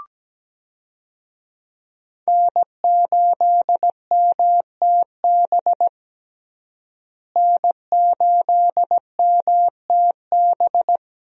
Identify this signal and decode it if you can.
Morse code, heard as a keyed tone, about 17 wpm, 700 Hz. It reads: N8MTB N8MTB
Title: USA Call Signs - 5 Characters - Repeated 3x - 17wpm